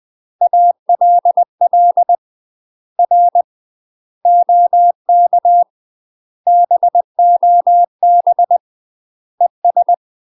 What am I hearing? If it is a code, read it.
Morse code, heard as a keyed tone, about 20 wpm, 700 Hz. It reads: ALL R OK BOB ES
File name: UK MASTER QSOs TRAINING/20WPM